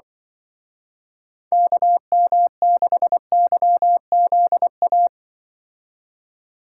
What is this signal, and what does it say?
Morse code, heard as a keyed tone, about 24 wpm, 700 Hz. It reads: KM6YZA
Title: Speed Racing - USA Call Signs 20wpm